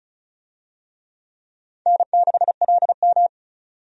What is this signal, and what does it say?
Morse code, heard as a keyed tone, about 36 wpm, 700 Hz. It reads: N6LM
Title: USA Call Signs - 4 Characters - 35wpm